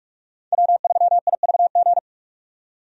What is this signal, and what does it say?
Morse code, heard as a keyed tone, about 46 wpm, 700 Hz. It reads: W3IVG
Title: USA Call Signs - 5 Characters - 45wpm